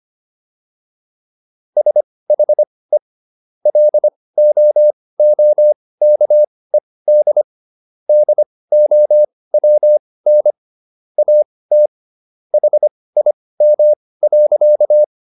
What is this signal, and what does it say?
Morse code, heard as a keyed tone, about 25 wpm, spaced as for 18 wpm, 600 Hz. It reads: SHE LOOKED DOWN AT HIM.
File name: SS123 18